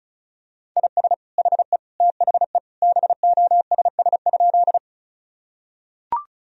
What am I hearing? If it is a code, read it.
Morse code, heard as a keyed tone, about 36 wpm, 700 Hz. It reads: IS HE THE BOSS?
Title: Sentences from Top 1000 Words - Review Part 4 - 35wpm